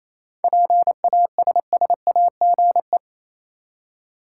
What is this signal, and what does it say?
Morse code, heard as a keyed tone, about 28 wpm, 700 Hz. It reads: PASSAGE
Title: Seven Letter Words - 28wpm